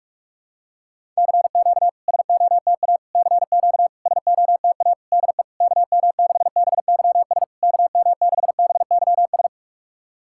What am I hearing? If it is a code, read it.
Morse code, heard as a keyed tone, about 46 wpm, 700 Hz. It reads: CQ SOTA CQ SOTA DE KM6BYS KM6BYS
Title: SOTA QSO Practice 45wpm